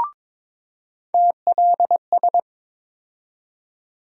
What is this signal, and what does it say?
Morse code, heard as a keyed tone, about 22 wpm, 700 Hz. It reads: TLS
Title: USA Call Sign Postfixes - 3 Letters 22wpm